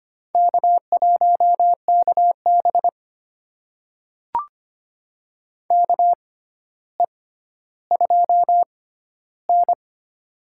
Morse code, heard as a keyed tone, about 25 wpm, spaced as for 13 wpm, 700 Hz. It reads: K1KB KE2N
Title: USA Call Signs - 4 Characters - 6x Character Spacing - 25wpm